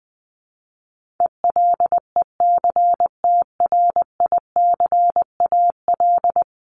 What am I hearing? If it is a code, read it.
Morse code, heard as a keyed tone, about 20 wpm, 700 Hz. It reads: ELECTRICAL
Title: Top 751-1000 Words 20wpm